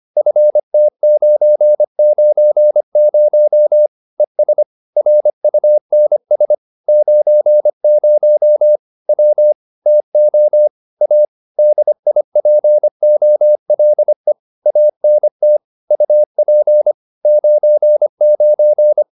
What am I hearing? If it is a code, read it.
Morse code, heard as a keyed tone, about 25 wpm, 600 Hz. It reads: FT990 ES RUNS 90 W TO A DIPOLE ANT UP 99